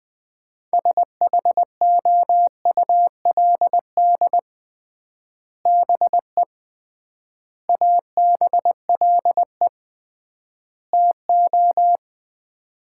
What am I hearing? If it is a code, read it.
Morse code, heard as a keyed tone, about 21 wpm, 700 Hz. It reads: SHOULD BE ABLE TO
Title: 3x Word-Spacing - Sets of 4 Words - 20wpm